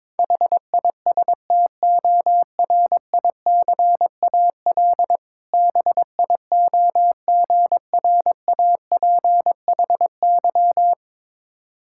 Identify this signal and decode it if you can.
Morse code, heard as a keyed tone, about 22 wpm, 700 Hz. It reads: HISTORICAL BIOGRAPHY
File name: Difficult Words with Hints - 22wpm